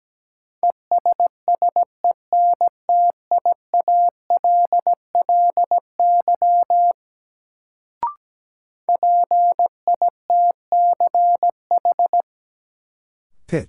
Morse code, heard as a keyed tone, about 17 wpm, 700 Hz. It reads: ESSENTIALLY PITCH
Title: Sentences from Top 1300 Words - New Word Drill - 17wpm